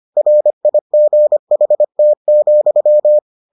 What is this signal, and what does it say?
Morse code, heard as a keyed tone, about 25 wpm, 600 Hz. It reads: RIGHT,